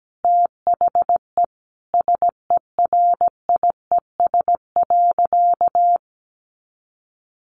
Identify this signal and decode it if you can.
Morse code, heard as a keyed tone, about 17 wpm, 700 Hz. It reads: THE SERIES.
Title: Sentences from Top 1200 Words 17wpm